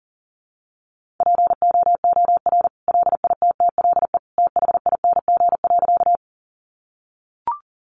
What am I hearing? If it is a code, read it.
Morse code, heard as a keyed tone, about 40 wpm, 700 Hz. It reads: POOR LITTLE THING.
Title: Sentences from Top 1000 Words 40wpm